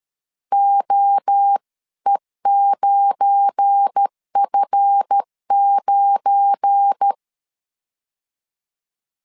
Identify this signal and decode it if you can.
Morse code, heard as a keyed tone, about 13 wpm, 800 Hz. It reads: OE9F9
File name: LEZIONI / 7 GF9Z / 7 GF9 12wpm 2319